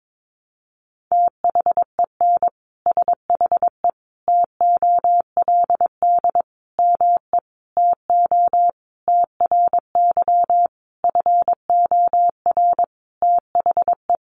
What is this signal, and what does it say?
Morse code, heard as a keyed tone, about 22 wpm, 700 Hz. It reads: THEN SHE TOLD ME TO TRY FOR THE
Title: Sentences from Top 400 Words 22wpm